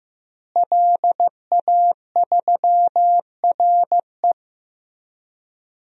Morse code, heard as a keyed tone, about 15 wpm, 700 Hz. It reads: LA3RE
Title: DXCC-100 Call Signs - 5 Characters - Repeated 2x - 15wpm